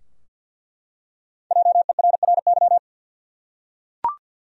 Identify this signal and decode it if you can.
Morse code, heard as a keyed tone, about 46 wpm, 700 Hz. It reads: JERRY
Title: USA Common Names - Max 5 Letters - 50wpm